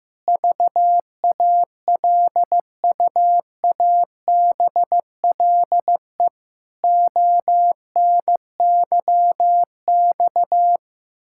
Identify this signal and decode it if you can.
Morse code, heard as a keyed tone, about 16 wpm, 700 Hz. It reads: VALUABLE ONYX
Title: Pangram Challenge v2 - 15wpm